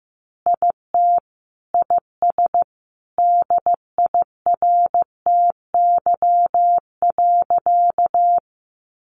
Morse code, heard as a keyed tone, about 15 wpm, 700 Hz. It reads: IT IS DIRTY.
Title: Additional Easy Phrases 15wpm